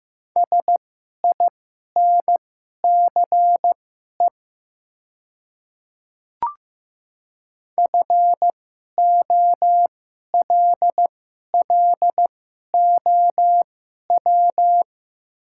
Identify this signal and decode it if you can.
Morse code, heard as a keyed tone, about 15 wpm, spaced as for 11 wpm, 690 Hz. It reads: SINCE FOLLOW
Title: 2x Character Spacing - Top 500 Words - 15wpm